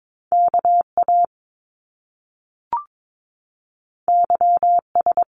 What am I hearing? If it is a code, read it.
Morse code, heard as a keyed tone, about 22 wpm, 700 Hz. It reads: KA YS
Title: Top Two Letter Combinations in Words 22wpm